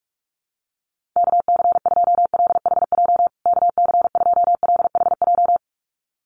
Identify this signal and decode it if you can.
Morse code, heard as a keyed tone, about 44 wpm, 700 Hz. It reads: KC2LHJ KC2LHJ
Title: USA Call Signs Repeated 2x 45wpm